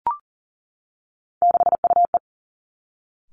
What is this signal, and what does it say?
Morse code, heard as a keyed tone, about 40 wpm, 700 Hz. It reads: 6UE
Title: Three Challenging Letter-Number Combinations 40wpm